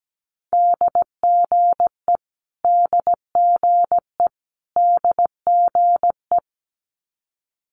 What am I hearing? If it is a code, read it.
Morse code, heard as a keyed tone, about 17 wpm, 700 Hz. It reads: DGE DGE DGE
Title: Multi-letter Phonemes Repeated 3x 17wpm